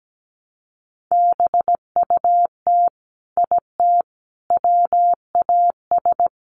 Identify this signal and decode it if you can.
Morse code, heard as a keyed tone, about 17 wpm, 700 Hz. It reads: BUT IT WAS
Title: Sentences from Top 1200 Words 17wpm